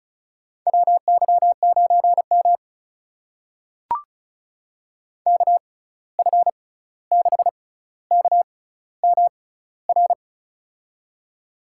Morse code, heard as a keyed tone, about 34 wpm, spaced as for 16 wpm, 700 Hz. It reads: WY9M KF6KMR
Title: USA Call Signs - 6x Character Spacing - 35wpm